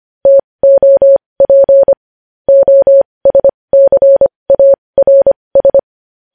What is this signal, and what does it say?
Morse code, heard as a keyed tone, about 25 wpm, spaced as for 20 wpm, 550 Hz. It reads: TOP OSCARS
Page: news-headlines-in-morse-code-at-20-wpm